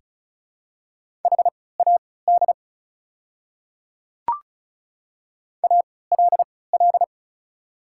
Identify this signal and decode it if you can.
Morse code, heard as a keyed tone, about 35 wpm, spaced as for 20 wpm, 690 Hz. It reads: HAD ALL
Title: 3x Character Spacing - Top 100 Words - 35wpm